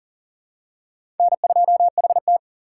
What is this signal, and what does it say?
Morse code, heard as a keyed tone, about 37 wpm, 700 Hz. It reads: N2HT
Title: USA Call Signs - 4x Character Spacing - 40wpm